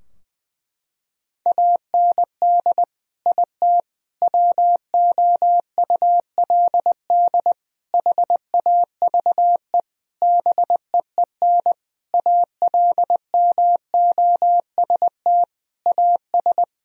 Morse code, heard as a keyed tone, about 20 wpm, 700 Hz. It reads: AND IT WOULD HAVE BEEN ALMOST AS
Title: Sentences from Top 1000 Words - Review Part 4 - 20wpm